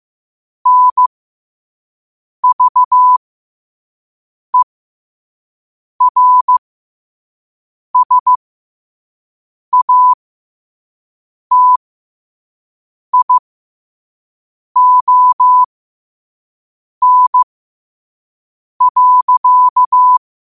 Morse code, heard as a keyed tone, about 15 wpm, spaced as for 5 wpm, 1000 Hz. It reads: NVERSATION.
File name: qotdfiles / 5wpm